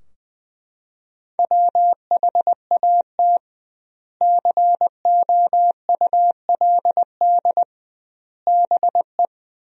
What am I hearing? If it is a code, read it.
Morse code, heard as a keyed tone, about 21 wpm, 700 Hz. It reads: WHAT COULD BE
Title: 2x Word-Spacing - Sentences from Top 400 Words - 20wpm